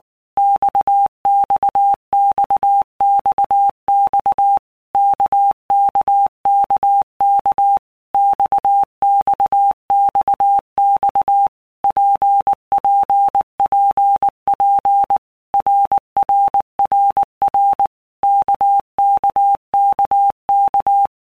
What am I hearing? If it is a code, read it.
Morse code, heard as a keyed tone, about 19 wpm, 800 Hz. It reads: XXXXX KKKK XXXX PPPP RRRR KKKK
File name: All Groups 1-minute Blocks / Groups@18 / RKPX